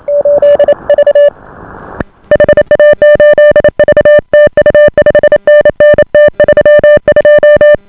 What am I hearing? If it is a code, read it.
Morse code, heard as a keyed tone, about 28 wpm, 590 Hz. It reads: 8V HA8V TU 5NN T32